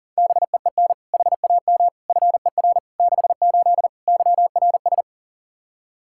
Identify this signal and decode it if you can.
Morse code, heard as a keyed tone, about 40 wpm, 700 Hz. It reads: BEEN HAM FER 68 YRS
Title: Practice QSO - Exchange 3 - More Info - 40wpm